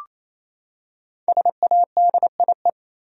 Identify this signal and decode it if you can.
Morse code, heard as a keyed tone, about 27 wpm, 700 Hz. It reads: SADIE
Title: USA Common Names - Max 5 Letters - 28wpm